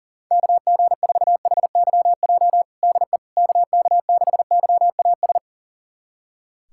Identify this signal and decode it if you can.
Morse code, heard as a keyed tone, about 40 wpm, 700 Hz. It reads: KG4HYJ DE KK6YAS
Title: USA Callsign DE Callsign - 6 Characters - 40wpm